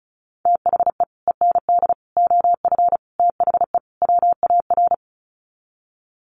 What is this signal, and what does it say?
Morse code, heard as a keyed tone, about 34 wpm, 700 Hz. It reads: THE END OF THE WAR
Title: Sets of 5 Words 35wpm